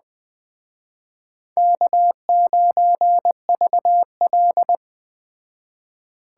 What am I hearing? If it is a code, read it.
Morse code, heard as a keyed tone, about 20 wpm, 700 Hz. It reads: K9VL
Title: Speed Racing - USA 1x2 Call Signs 17wpm